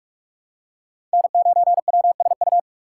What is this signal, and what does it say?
Morse code, heard as a keyed tone, about 44 wpm, 700 Hz. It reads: N 9 W S U
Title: USA Call Signs - 5 Characters - 6x Character Spacing - 45wpm